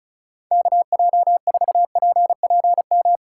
Morse code, heard as a keyed tone, about 39 wpm, 700 Hz. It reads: KJ4PPM
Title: Speed Racing - USA Call Signs 35wpm